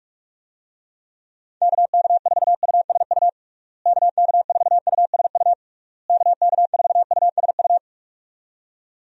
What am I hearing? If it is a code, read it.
Morse code, heard as a keyed tone, about 43 wpm, 700 Hz. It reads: KK4USU KK4USU KK4USU
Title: USA Call Signs - 6 Characters - Repeated 3x - 45wpm